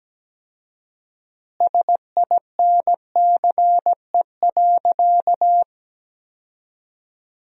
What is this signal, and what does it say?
Morse code, heard as a keyed tone, about 17 wpm, 700 Hz. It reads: SINCE.
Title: 4x Word-Spacing - Sentences from Top 400 Words - 17wpm